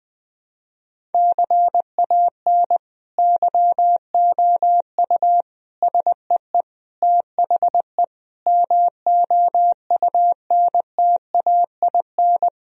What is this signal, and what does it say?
Morse code, heard as a keyed tone, about 20 wpm, 700 Hz. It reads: CAN YOU SEE THE MOUNTAIN
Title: Sentences from Top 1000 Words - Review Part 1 - 20wpm